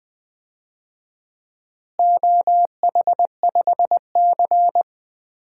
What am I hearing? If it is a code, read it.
Morse code, heard as a keyed tone, about 21 wpm, 700 Hz. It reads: OH5C
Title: DXCC-100 Call Signs - 4 Characters - Repeated 2x - 20wpm